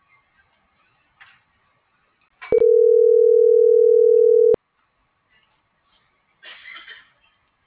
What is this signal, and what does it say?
Ringback tone (North America), 440 + 480 Hz.
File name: CENTRAL SOTOVA / out alejandro.rodriguez - .1814